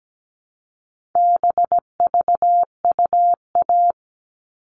Morse code, heard as a keyed tone, about 17 wpm, 700 Hz. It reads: BVUA
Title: All Cut Numbers - 4-Digit 17wpm